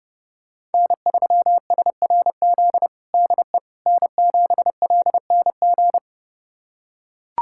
Morse code, heard as a keyed tone, about 30 wpm, 700 Hz. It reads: N3SRZ DE N7LNG
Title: USA Callsign DE Callsign - 5 Characters - 30wpm